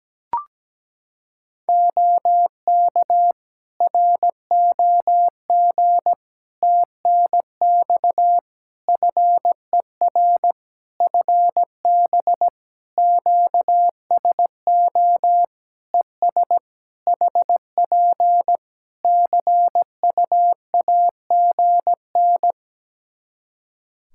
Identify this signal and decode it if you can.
Morse code, heard as a keyed tone, about 17 wpm, 700 Hz. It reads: OK ROG TNX FER FB QSO ES HP CUAGN
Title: Practice QSO - Exchange 4 - Ending - 17wpm